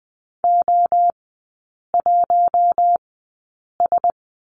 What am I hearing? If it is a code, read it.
Morse code, heard as a keyed tone, about 20 wpm, spaced as for 8 wpm, 700 Hz. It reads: O1S
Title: 087 Lesson 15e - Number 5 - Three Characters 20wpm